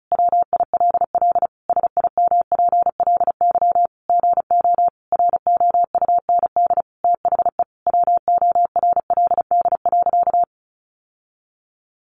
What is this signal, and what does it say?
Morse code, heard as a keyed tone, about 34 wpm, 700 Hz. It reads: WILL SIMPLY GO ROUND THE WORLD.
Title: Sentences from Top 600 Words 35wpm